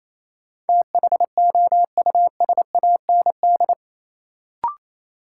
Morse code, heard as a keyed tone, about 28 wpm, 700 Hz. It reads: THOUSAND
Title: Sentences from Top 600 Words - New Word Drill - 28wpm